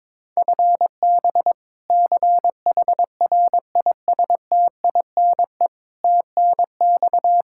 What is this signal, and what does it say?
Morse code, heard as a keyed tone, about 21 wpm, 700 Hz. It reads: FB CHRISTINE TNX
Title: Practice QSO - Exchange 3 - More Info - 22wpm